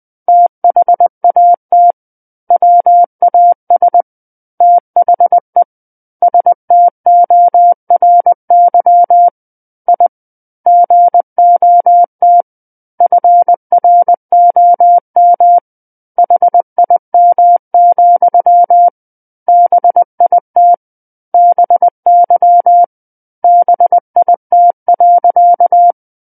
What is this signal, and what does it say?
Morse code, heard as a keyed tone, about 20 wpm, 700 Hz. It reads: THAT WAS THE STORY I GOT FROM HIM, BIT BY BIT.
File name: 20 WPM